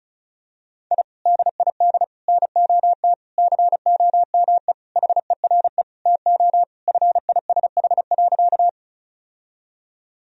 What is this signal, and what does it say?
Morse code, heard as a keyed tone, about 34 wpm, 700 Hz. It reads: I DID NOT COME HERE TO FISH.
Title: Sentences from Top 1000 Words - Review Part 1 - 35wpm